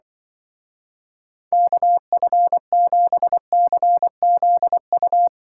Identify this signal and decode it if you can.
Morse code, heard as a keyed tone, about 25 wpm, 700 Hz. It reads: KF7CZU